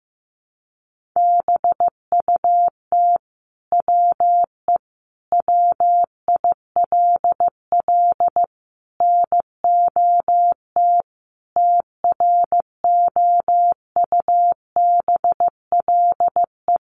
Morse code, heard as a keyed tone, about 15 wpm, 700 Hz. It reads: BUT WE WILL NOT TROUBLE
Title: Sentences from Top 1200 Words 15wpm